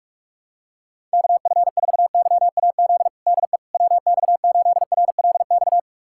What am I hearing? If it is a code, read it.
Morse code, heard as a keyed tone, about 46 wpm, 700 Hz. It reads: KF4YAZ DE WX8RLX
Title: USA Callsign DE Callsign 45wpm